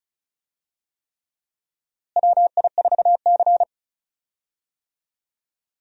Morse code, heard as a keyed tone, about 36 wpm, 700 Hz. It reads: WI4C
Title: USA Call Signs - 4 Characters - Repeated 3x - 35wpm